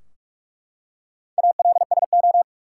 Morse code, heard as a keyed tone, about 43 wpm, 700 Hz. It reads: ALSO
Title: Four Letter Words - 45wpm